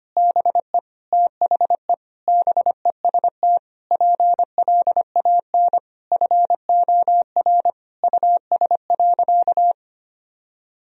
Morse code, heard as a keyed tone, about 25 wpm, 700 Hz. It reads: BE THE BEST PLAN FOR US.
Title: Sentences from Top 1000 Words - Review Part 2 - 25wpm